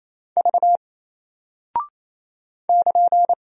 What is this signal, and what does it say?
Morse code, heard as a keyed tone, about 28 wpm, 700 Hz. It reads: V <KN>
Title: Single Character (Letter:Number:Punctuation:Prosign) Alt 28wpm